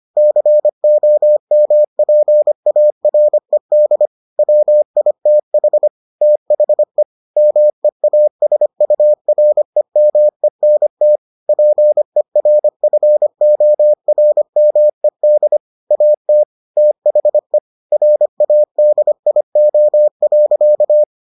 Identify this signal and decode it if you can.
Morse code, heard as a keyed tone, about 25 wpm, 600 Hz. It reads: COMPARED WITH THE MEASUREMENT PERFORMED AT THE RADIO.